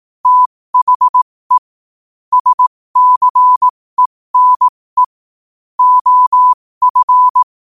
Morse code, heard as a keyed tone, about 18 wpm, spaced as for 15 wpm, 1000 Hz. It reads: THE SCENE OF
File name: qotdfiles / 15wpm